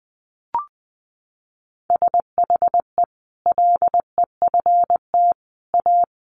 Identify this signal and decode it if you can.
Morse code, heard as a keyed tone, about 20 wpm, 700 Hz. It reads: SHE LEFT A
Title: Sentences from Top 1100 Words 20wpm